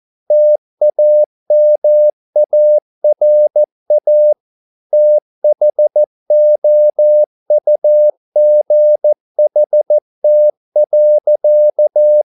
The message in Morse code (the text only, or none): TAMARA THOUGHT.